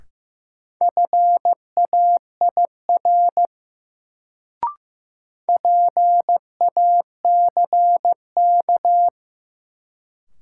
Morse code, heard as a keyed tone, about 15 wpm, 700 Hz. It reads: FAIR PACK
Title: Sentences from Top 1300 Words - New Word Drill Alt - 15wpm